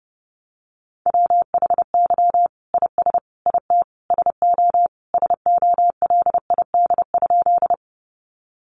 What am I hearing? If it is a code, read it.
Morse code, heard as a keyed tone, about 28 wpm, 700 Hz. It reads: WHY IS IT SO SOLID?
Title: Sentences from Top 1500 Words 30wpm